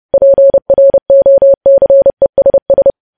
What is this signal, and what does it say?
Morse code, heard as a keyed tone, about 30 wpm, 550 Hz. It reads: PROCESS
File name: podcast / News headlines-1x-30x30